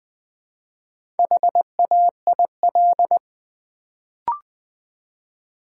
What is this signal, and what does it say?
Morse code, heard as a keyed tone, about 20 wpm, 700 Hz. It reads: HAIL
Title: Build A Word - 1 Letter At A Time - 4-Letter Words - 20wpm